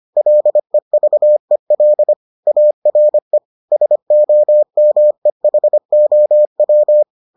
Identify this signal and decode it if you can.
Morse code, heard as a keyed tone, about 25 wpm, 600 Hz. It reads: LEVEL ARE SOMEHOW